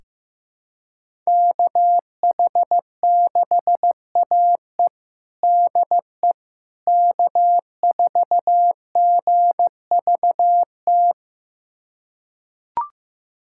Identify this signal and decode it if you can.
Morse code, heard as a keyed tone, about 15 wpm, 700 Hz. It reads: KH6AE DE K4GVT
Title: USA Callsign DE Callsign - 5 Characters - 15wpm